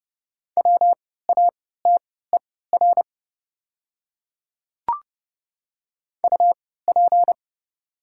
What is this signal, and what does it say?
Morse code, heard as a keyed tone, about 30 wpm, spaced as for 17 wpm, 700 Hz. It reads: WATER UP
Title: 3x Character Spacing - Top 100 Words - 30wpm